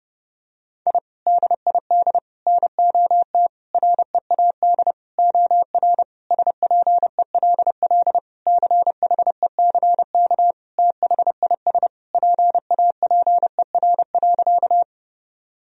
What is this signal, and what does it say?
Morse code, heard as a keyed tone, about 31 wpm, 700 Hz. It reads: I DID NOT READ OR SPELL CHECK THIS PAPER.
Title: Sentences from Top 1000 Words - Review Part 3 - 30wpm